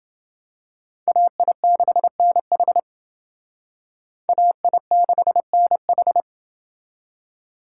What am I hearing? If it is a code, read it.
Morse code, heard as a keyed tone, about 28 wpm, 700 Hz. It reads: AI6NH AI6NH